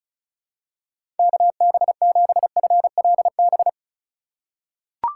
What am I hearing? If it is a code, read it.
Morse code, heard as a keyed tone, about 36 wpm, 700 Hz. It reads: KB7FLB
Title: USA Call Signs - 6 Characters - 4x Character Spacing - 35wpm